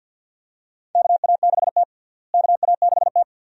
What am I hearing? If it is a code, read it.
Morse code, heard as a keyed tone, about 52 wpm, 700 Hz. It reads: KA6T KA6T
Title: USA Call Signs - 4 Characters - Repeated 2x - 50wpm